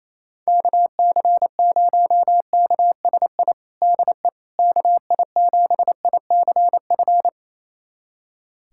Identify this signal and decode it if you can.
Morse code, heard as a keyed tone, about 28 wpm, 700 Hz. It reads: KC0KSI DE KI7ICF
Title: USA Callsign DE Callsign - 6 Characters - 28wpm